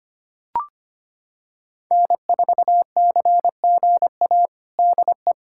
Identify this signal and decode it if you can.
Morse code, heard as a keyed tone, about 25 wpm, 700 Hz. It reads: N4CGA DE
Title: USA Callsign DE Callsign - 5 Characters - 25wpm